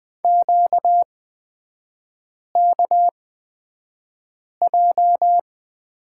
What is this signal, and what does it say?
Morse code, heard as a keyed tone, about 20 wpm, spaced as for 5 wpm, 700 Hz. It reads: QKJ
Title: 236 Lesson 38e - Letter q - Three Characters 20wpm